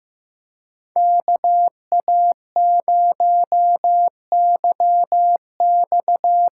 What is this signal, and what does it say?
Morse code, heard as a keyed tone, about 15 wpm, 700 Hz. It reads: KA0YX
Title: USA Call Signs - 6 Characters - 6x Character Spacing - 15wpm